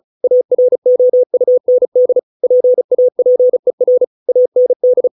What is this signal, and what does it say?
Morse code, heard as a keyed tone, about 35 wpm, 500 Hz. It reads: AROUND PAPER AND